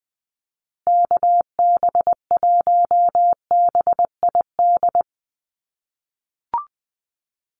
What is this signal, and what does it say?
Morse code, heard as a keyed tone, about 20 wpm, 700 Hz. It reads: KB1BID
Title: USA Call Signs Repeated 2x 20wpm